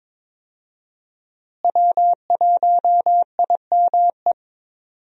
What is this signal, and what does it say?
Morse code, heard as a keyed tone, about 22 wpm, 700 Hz. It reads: W1IME
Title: USA Call Signs - 5 Characters - Repeated 2x - 22wpm